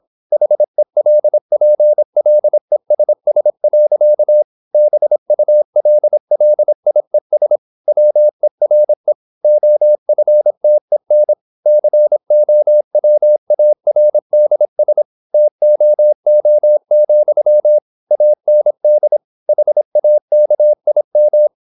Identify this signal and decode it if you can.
Morse code, heard as a keyed tone, about 26 wpm, 600 Hz. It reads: HELPLESS. BULLIES WERE OFTEN COWARDS TOO, AND HAKIM
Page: Never 26wpm | CW med Gnister